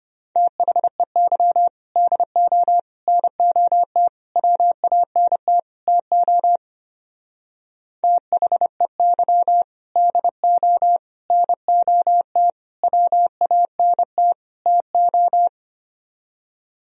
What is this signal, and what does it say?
Morse code, heard as a keyed tone, about 28 wpm, 700 Hz. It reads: THEY DO NOT WANT TO THEY DO NOT WANT TO
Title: Speed-Racing - Sets of 5 Words 25wpm